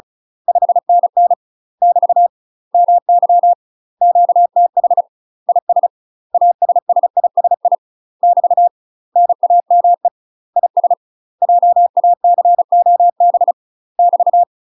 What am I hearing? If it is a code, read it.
Morse code, heard as a keyed tone, about 36 wpm, 700 Hz. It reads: 5NN <BT> MY QTH IS ASSISI <BT> NAME IS JACOB <BT>
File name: UK MASTER QSOs TRAINING / 35WPM / UK MASTER QSOs 0041